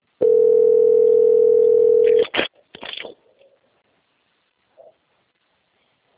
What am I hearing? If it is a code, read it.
Ringback tone (North America), 440 + 480 Hz.
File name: out- - - - - .11697